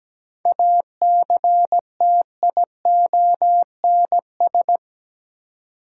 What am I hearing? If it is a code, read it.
Morse code, heard as a keyed tone, about 17 wpm, 700 Hz. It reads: ACTIONS
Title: Sentences from Top 1000 Words - Review Part 6 - New Word Drill - 17wpm